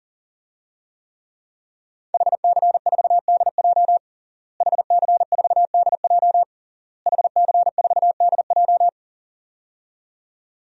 Morse code, heard as a keyed tone, about 40 wpm, 700 Hz. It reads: HC4DJ HC4DJ HC4DJ
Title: DXCC-100 Call Signs - 5 Characters - Repeated 3x - 40wpm